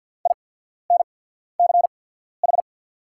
Morse code, heard as a keyed tone, about 48 wpm, spaced as for 14 wpm, 690 Hz. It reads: INCH
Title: 8x Character Spacing - Top 500 Words - 50wpm